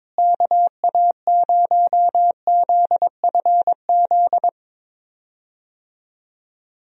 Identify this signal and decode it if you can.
Morse code, heard as a keyed tone, about 22 wpm, 700 Hz. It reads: KA0ZFZ
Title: USA Call Signs - 6 Characters - Repeated 2x - 22wpm